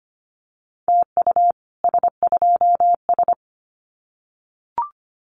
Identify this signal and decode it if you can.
Morse code, heard as a keyed tone, about 25 wpm, 700 Hz. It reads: TU S2S
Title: Common QSO Phrases Encore 25wpm